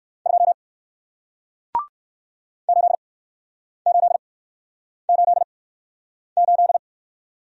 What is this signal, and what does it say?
Morse code, heard as a keyed tone, about 61 wpm, spaced as for 9 wpm, 700 Hz. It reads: 2 8888